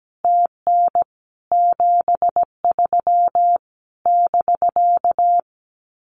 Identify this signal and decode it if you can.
Morse code, heard as a keyed tone, about 17 wpm, 700 Hz. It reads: TN 73 <BK>
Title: Common QSO Phrases Encore 17wpm